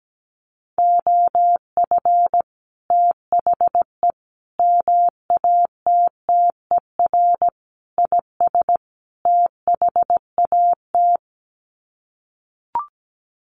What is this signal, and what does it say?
Morse code, heard as a keyed tone, about 17 wpm, 700 Hz. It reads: OF THE MATTER IS THAT
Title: Sets of 5 Words Encore 17wpm